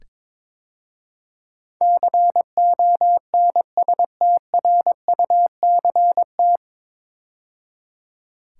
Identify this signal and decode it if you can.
Morse code, heard as a keyed tone, about 22 wpm, 700 Hz. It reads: CONSTRUCT
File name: 9 Letter Words - First 3 Letters Given - 22wpm